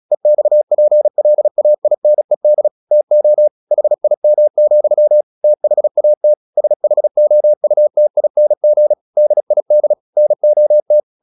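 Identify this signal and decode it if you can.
Morse code, heard as a keyed tone, about 36 wpm, 600 Hz. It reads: EXPLAINED TO HIM, THAT SHOUTING DID NOT
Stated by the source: Morse practice at 36 wpm, text of Ken Follett's novel Never, from CW med Gnister.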